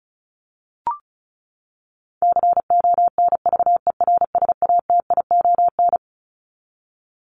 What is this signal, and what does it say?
Morse code, heard as a keyed tone, about 34 wpm, 700 Hz. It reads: CONVERSATION
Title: Top 1751-2000 Words 35wpm